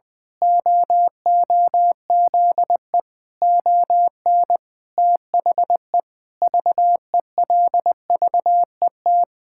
Morse code, heard as a keyed tone, about 20 wpm, 700 Hz. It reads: OOZE ON THE VELVET
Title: Pangram Challenge v2 - 20wpm